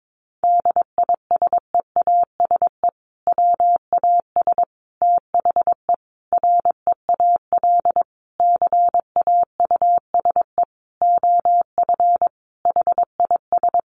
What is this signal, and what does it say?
Morse code, heard as a keyed tone, about 22 wpm, 700 Hz. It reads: DISEASE WAS THE REAL CAUSE OF HIS
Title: Sentences from Top 1100 Words 22wpm